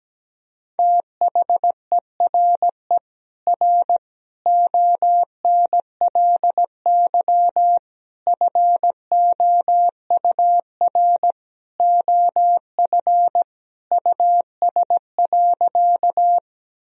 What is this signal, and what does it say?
Morse code, heard as a keyed tone, about 17 wpm, 700 Hz. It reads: THERE R ONLY FOUR OF US.
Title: Sentences from Top 300 Words and Abbreviations 17wpm